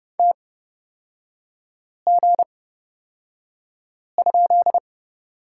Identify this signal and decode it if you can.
Morse code, heard as a keyed tone, about 29 wpm, spaced as for 5 wpm, 700 Hz. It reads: TG?
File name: Lesson 38e - Letter q - Three Characters 30wpm